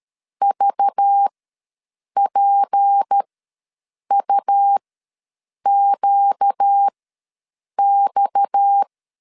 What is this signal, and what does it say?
Morse code, heard as a keyed tone, about 13 wpm, spaced as for 7 wpm, 800 Hz. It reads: VPUQX
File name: LEZIONI / 6 PKX0 / 6 PKX0 ws 12wpm 3448